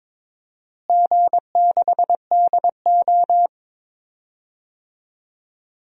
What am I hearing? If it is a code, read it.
Morse code, heard as a keyed tone, about 21 wpm, 700 Hz. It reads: G6DO
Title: DXCC-100 Call Signs - 4 Characters - Repeated 2x - 22wpm